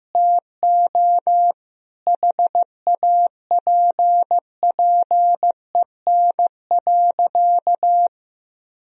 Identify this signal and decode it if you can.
Morse code, heard as a keyed tone, about 16 wpm, 700 Hz. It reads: TO HAPPEN.
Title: Sentences from Top 1000 Words - Review Part 4 - 15wpm